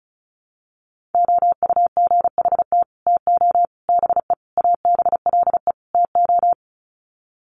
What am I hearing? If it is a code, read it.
Morse code, heard as a keyed tone, about 34 wpm, 700 Hz. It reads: OUGHT TO BE ABLE TO
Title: Sets of 5 Words Encore2 35wpm